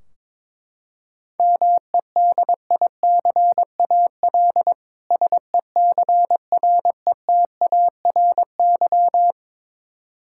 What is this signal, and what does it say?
Morse code, heard as a keyed tone, about 22 wpm, 700 Hz. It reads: MEDICAL SECRETARY
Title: K1USN SST QSO Practice 22wpm